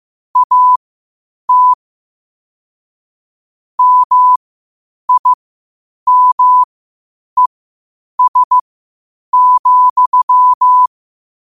Morse code, heard as a keyed tone, about 15 wpm, spaced as for 8 wpm, 1000 Hz. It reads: AT MIMES,
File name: qotdfiles / 8wpm